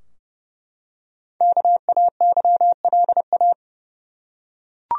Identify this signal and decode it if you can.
Morse code, heard as a keyed tone, about 30 wpm, 700 Hz. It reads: KAYLA
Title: USA Common Names - 30wpm